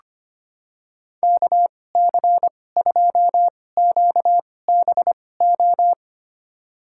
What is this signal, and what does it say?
Morse code, heard as a keyed tone, about 25 wpm, spaced as for 18 wpm, 700 Hz. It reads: KC2QBO
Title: USA Call Signs - 2x Character Spacing - 25wpm